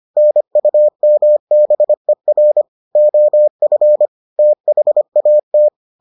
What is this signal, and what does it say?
Morse code, heard as a keyed tone, about 25 wpm, 600 Hz. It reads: NUMBER OF THAT